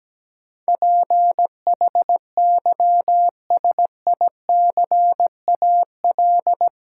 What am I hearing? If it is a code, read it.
Morse code, heard as a keyed tone, about 17 wpm, 700 Hz. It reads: PHYSICAL
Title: Sentences from Top 1400 Words - New Word Drill Alt - 17wpm